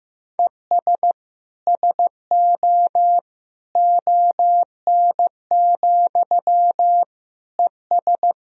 Morse code, heard as a keyed tone, about 15 wpm, 700 Hz. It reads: ES SO ON, ES
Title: Sentences from Top 100 Words and Abbreviations 15wpm